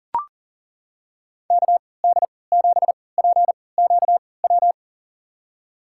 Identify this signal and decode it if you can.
Morse code, heard as a keyed tone, about 40 wpm, spaced as for 23 wpm, 700 Hz. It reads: KD7PQW
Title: USA Call Signs - 6 Characters - 3x Character Spacing - 40wpm